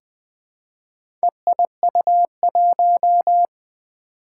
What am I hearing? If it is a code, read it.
Morse code, heard as a keyed tone, about 20 wpm, 700 Hz. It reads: EIU1
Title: Four Challenging Letter-Number Combinations 20wpm